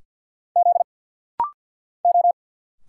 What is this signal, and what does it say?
Morse code, heard as a keyed tone, about 50 wpm, 700 Hz. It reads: Z O
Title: Single Letters - Rapid-Fire 50wpm